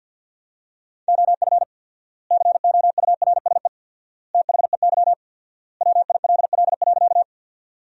Morse code, heard as a keyed tone, about 53 wpm, 700 Hz. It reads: OF COURSE THEY WILL.
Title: 4x Word-Spacing - Sentences from Top 500 Words - 50wpm